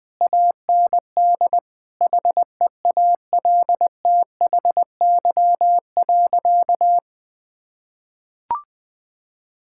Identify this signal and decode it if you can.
Morse code, heard as a keyed tone, about 20 wpm, 700 Hz. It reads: AND HEALTHY.
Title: Sentences from Top 1000 Words - Review Part 5 - 20wpm